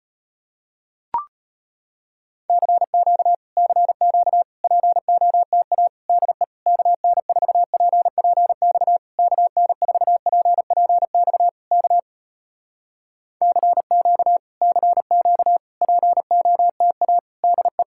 Morse code, heard as a keyed tone, about 36 wpm, 700 Hz. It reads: CQ CQ POTA DE KN4PPX KN4PPX K CQ CQ POTA DE